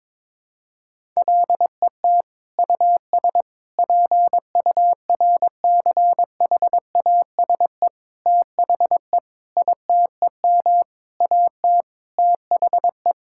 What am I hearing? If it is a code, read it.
Morse code, heard as a keyed tone, about 22 wpm, 700 Hz. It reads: LET US PURCHASE THE ITEM AT THE
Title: Sentences from Top 1000 Words - Review Part 6 - 22wpm